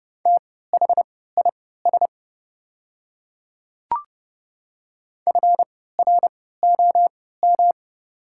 Morse code, heard as a keyed tone, about 30 wpm, spaced as for 17 wpm, 700 Hz. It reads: THIS FROM
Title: 3x Character Spacing - Top 100 Words - 30wpm